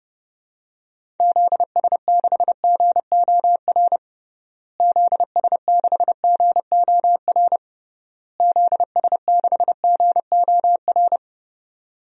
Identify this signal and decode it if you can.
Morse code, heard as a keyed tone, about 30 wpm, 700 Hz. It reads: ZS6GOR ZS6GOR ZS6GOR
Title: DXCC-100 Call Signs - 6 Characters - Repeated 3x - 30wpm